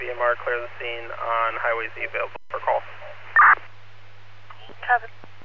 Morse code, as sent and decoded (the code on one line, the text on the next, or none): none